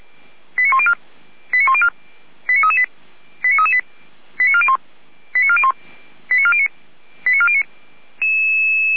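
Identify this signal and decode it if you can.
Five-tone selective call (ZVEI), heard as quick runs of five tones, tones of about 70 ms. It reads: 89104 89104 89338 89338 89401 89401 89409 89409
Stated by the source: ZVEI five-tone selective call received by SDR, archived by the Orlamünde fire brigade (Feuerwehr Orlamünde).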